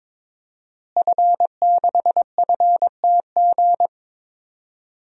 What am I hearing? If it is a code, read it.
Morse code, heard as a keyed tone, about 22 wpm, 700 Hz. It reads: F6FTG
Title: DXCC-100 Call Signs - 5 Characters - Repeated 2x - 22wpm